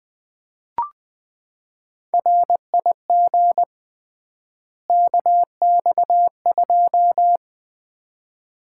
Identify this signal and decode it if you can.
Morse code, heard as a keyed tone, about 20 wpm, 700 Hz. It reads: RIG KX2
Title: Common QSO Phrases - 3x Word-Spacing - 20wpm